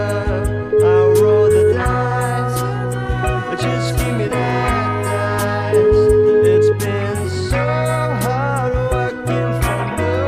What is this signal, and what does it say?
Ringback tone (Europe, 425 Hz), with noise.